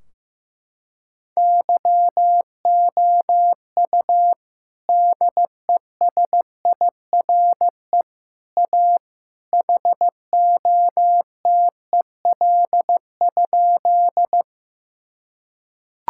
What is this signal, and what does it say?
Morse code, heard as a keyed tone, about 16 wpm, 700 Hz. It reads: YOU DESIRE A HOTEL?
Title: Sentences from Top 1600 Words 15wpm